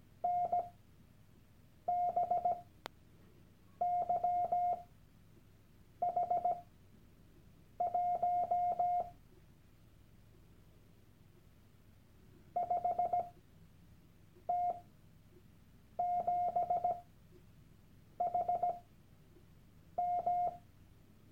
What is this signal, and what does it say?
Morse code, heard as a keyed tone, about 17 wpm, spaced as for 6 wpm, 710 Hz. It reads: NBYH1 5T7HM
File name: Workshop Dateien / CW-Online-Kurs / Lektion 001